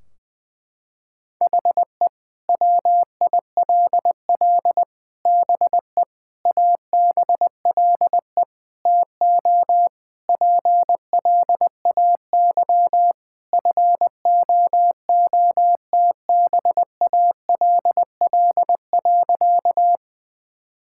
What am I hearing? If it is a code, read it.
Morse code, heard as a keyed tone, about 20 wpm, 700 Hz. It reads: HE WILL BE ABLE TO PLAY FOOTBALL.
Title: Sentences from Top 1600 Words 20wpm